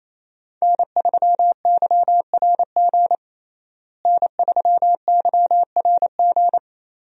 Morse code, heard as a keyed tone, about 28 wpm, 700 Hz. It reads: N3YRG N3YRG
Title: USA Call Signs - 5 Characters - Repeated 3x - 28wpm